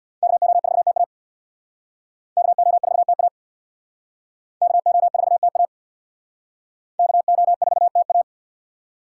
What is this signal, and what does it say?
Morse code, heard as a keyed tone, about 67 wpm, 700 Hz. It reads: KO4TA KO4TA KO4TA KO4TA